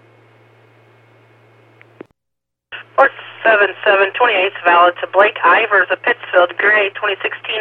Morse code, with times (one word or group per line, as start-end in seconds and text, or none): none